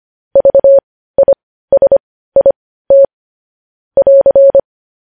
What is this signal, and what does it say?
Morse code, heard as a keyed tone, about 25 wpm, spaced as for 15 wpm, 550 Hz. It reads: VISIT <AR>
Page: news-headlines-in-morse-code-at-15-wpm